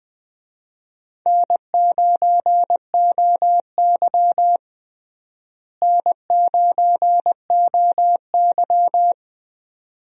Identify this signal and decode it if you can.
Morse code, heard as a keyed tone, about 20 wpm, 700 Hz. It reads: N9OY N9OY
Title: USA Call Signs - 4 Characters - Repeated 2x - 20wpm